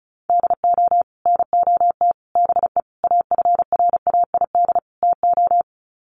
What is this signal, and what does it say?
Morse code, heard as a keyed tone, about 34 wpm, 700 Hz. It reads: DO NOT BE AFRAID TO
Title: Sets of 5 Words Encore2 35wpm